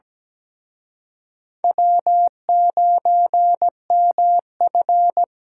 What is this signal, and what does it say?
Morse code, heard as a keyed tone, about 20 wpm, 700 Hz. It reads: W9MF
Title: Speed Racing - USA 1x2 Call Signs 17wpm